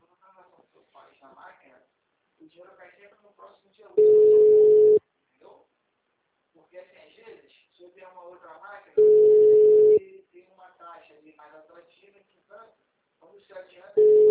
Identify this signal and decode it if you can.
Ringback tone (Europe, 425 Hz), with speech.